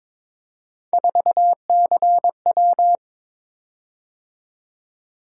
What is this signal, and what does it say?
Morse code, heard as a keyed tone, about 22 wpm, 700 Hz. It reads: 4CW
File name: practice / 22wpm / Three Letter-Number Combinations - 22wpm